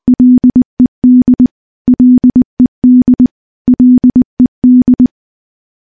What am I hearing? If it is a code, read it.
Morse code, heard as a keyed tone, about 21 wpm, 250 Hz. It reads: LED LED LED
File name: morse-training-20WPM